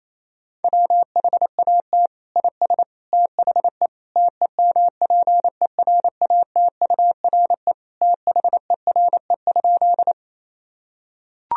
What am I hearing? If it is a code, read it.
Morse code, heard as a keyed tone, about 28 wpm, 700 Hz. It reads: WHAT IS THE TEMPERATURE THERE?
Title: Sentences from Top 1000 Words - Review Part 4 - 28wpm